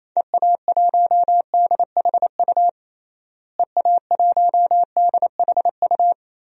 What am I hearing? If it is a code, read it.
Morse code, heard as a keyed tone, about 27 wpm, 700 Hz. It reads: EA1DHU EA1DHU
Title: DXCC-100 Call Signs - 6 Characters - Repeated 3x - 28wpm